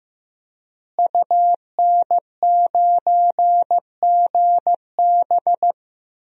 Morse code, heard as a keyed tone, about 15 wpm, 700 Hz. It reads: UN9GB
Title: DXCC-100 Call Signs - 5 Characters - Repeated 2x - 15wpm